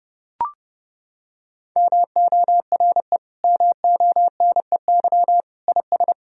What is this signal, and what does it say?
Morse code, heard as a keyed tone, about 30 wpm, 700 Hz. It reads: MORE MONEY IS
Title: Sentences from Top 1000 Words - Review Part 3 - 30wpm